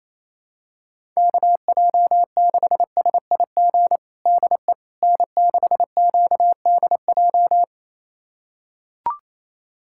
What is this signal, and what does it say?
Morse code, heard as a keyed tone, about 28 wpm, 700 Hz. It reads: KJ6SIG DE N6QDJ
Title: USA Callsign DE Callsign 28wpm